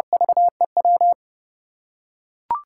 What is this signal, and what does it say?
Morse code, heard as a keyed tone, about 30 wpm, 700 Hz. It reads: VEW
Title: USA Call Signs - 6 Characters - 4x Character Spacing - 30wpm